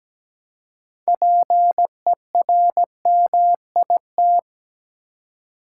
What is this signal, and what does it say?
Morse code, heard as a keyed tone, about 17 wpm, 700 Hz. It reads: PERMIT
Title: Six Letter Words - 17wpm